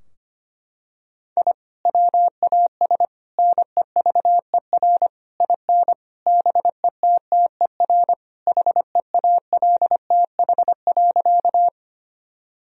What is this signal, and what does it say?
Morse code, heard as a keyed tone, about 25 wpm, 700 Hz. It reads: I WAS NEVER IN BETTER HEALTH.
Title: Sentences from Top 1000 Words - Review Part 3 - 25wpm